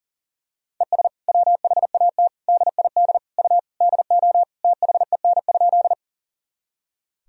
Morse code, heard as a keyed tone, about 40 wpm, 700 Hz. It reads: ES WHAT DID U DO THEN?
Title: Sentences from Top 100 Words and Abbreviations 40wpm